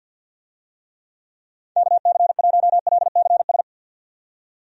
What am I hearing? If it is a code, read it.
Morse code, heard as a keyed tone, about 48 wpm, 700 Hz. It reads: KC1LCS
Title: USA Call Signs - 6 Characters - 50wpm